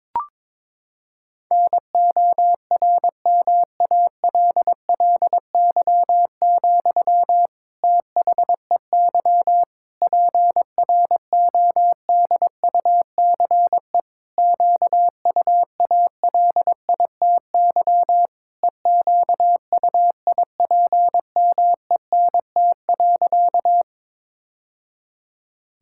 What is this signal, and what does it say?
Morse code, heard as a keyed tone, about 22 wpm, 700 Hz. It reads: NORMALLY, THEY PRODUCE QUALITY EQUIPMENT.
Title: Sentences from Top 1000 Words - Review Part 6 - 22wpm